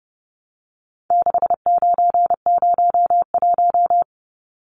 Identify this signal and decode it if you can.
Morse code, heard as a keyed tone, about 30 wpm, 700 Hz. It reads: 6901
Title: Numbers - 4-Digit 30wpm